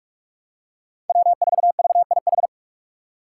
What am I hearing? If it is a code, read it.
Morse code, heard as a keyed tone, about 44 wpm, 700 Hz. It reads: W4VIH
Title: Code Groups - 5 Letter:Numbers - 45wpm